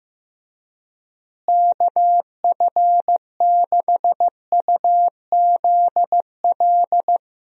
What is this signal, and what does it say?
Morse code, heard as a keyed tone, about 15 wpm, 700 Hz. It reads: KF6UZL
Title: USA Call Signs - 6 Characters - Repeated 2x - 15wpm